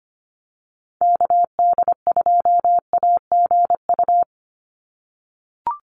Morse code, heard as a keyed tone, about 25 wpm, 700 Hz. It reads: KD2AGU
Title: USA Call Signs 25wpm